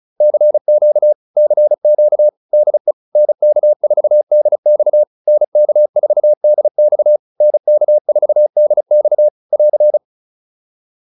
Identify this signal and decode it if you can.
Morse code, heard as a keyed tone, about 34 wpm, 600 Hz. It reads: CQ CQ DE NK4DX NK4DX NK4DX <AR>